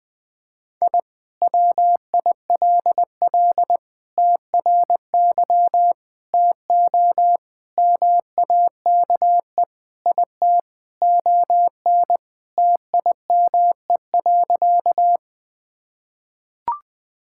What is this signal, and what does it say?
Morse code, heard as a keyed tone, about 20 wpm, 700 Hz. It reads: I WILL TRY TO MAKE IT ON TIME.
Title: Sentences with 3-Word Phrases - 20wpm